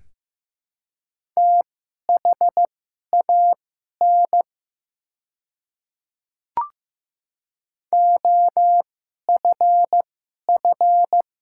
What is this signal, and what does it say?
Morse code, heard as a keyed tone, about 15 wpm, spaced as for 11 wpm, 700 Hz. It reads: THAN OFF
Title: 2x Character Spacing - Top 200 Words - 15wpm